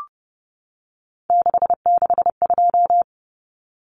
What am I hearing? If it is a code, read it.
Morse code, heard as a keyed tone, about 30 wpm, 700 Hz. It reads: 662
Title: Common Cut Numbers - 3-Digit 30wpm